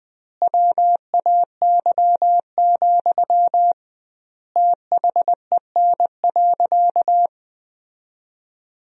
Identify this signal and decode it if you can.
Morse code, heard as a keyed tone, about 20 wpm, 700 Hz. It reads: WAY, THEN.
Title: 2x Word-Spacing - Sentences from Top 400 Words - 20wpm